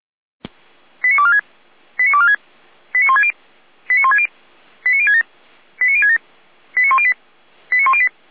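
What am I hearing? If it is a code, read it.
Five-tone selective call (ZVEI), heard as quick runs of five tones, tones of about 80 ms. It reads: 89246 89246 89160 89160 89067 89067 89108 89108
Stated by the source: Five-tone ZVEI selective call received on an SDR.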